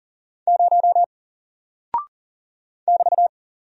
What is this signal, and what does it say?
Morse code, heard as a keyed tone, about 40 wpm, 700 Hz. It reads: 0 <BT>
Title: Single Character (Letter:Number:Punctuation:Prosign) Alt 40wpm